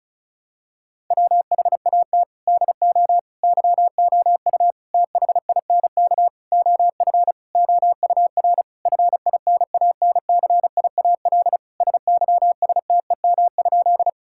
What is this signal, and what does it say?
Morse code, heard as a keyed tone, about 36 wpm, 700 Hz. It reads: WHAT DO YOU THINK OF OUR FINANCIAL SYSTEM?
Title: Sentences from Top 1000 Words - Review Part 5 - 35wpm